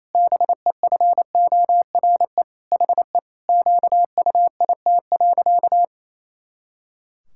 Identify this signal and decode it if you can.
Morse code, heard as a keyed tone, about 28 wpm, 700 Hz. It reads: BEFORE HE QUIT.
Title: Pangram Challenge v2 - 28wpm